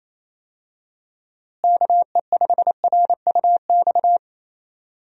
Morse code, heard as a keyed tone, about 28 wpm, 700 Hz. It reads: KE5RUX
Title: USA Call Signs - 6 Characters - Repeated 2x - 28wpm